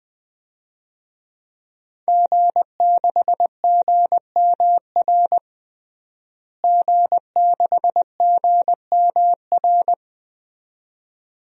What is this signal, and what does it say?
Morse code, heard as a keyed tone, about 20 wpm, 700 Hz. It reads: G6GMR G6GMR
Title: DXCC-100 Call Signs - 5 Characters - Repeated 2x - 20wpm